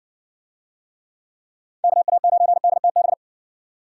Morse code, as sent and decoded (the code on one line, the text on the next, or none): -.- .- ----. -.. - -...
KA9DTB